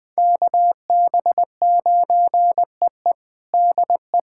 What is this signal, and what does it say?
Morse code, heard as a keyed tone, about 19 wpm, 700 Hz. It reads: KB9EE DE
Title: USA Callsign DE Callsign 20wpm